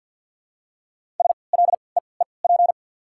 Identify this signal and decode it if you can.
Morse code, heard as a keyed tone, about 48 wpm, spaced as for 28 wpm, 700 Hz. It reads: SLEEP
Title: 3x Character Spacing - Top 500 Words - 50wpm